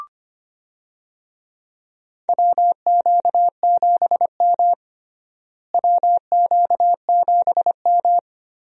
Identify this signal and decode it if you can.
Morse code, heard as a keyed tone, about 25 wpm, 700 Hz. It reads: WQ7M WQ7M
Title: USA Call Signs - 4 Characters - Repeated 3x - 25wpm